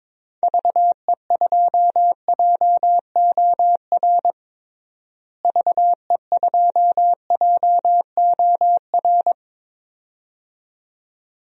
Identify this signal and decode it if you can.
Morse code, heard as a keyed tone, about 21 wpm, 700 Hz. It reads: VE2JOR VE2JOR
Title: DXCC-100 Call Signs - 6 Characters - Repeated 2x - 22wpm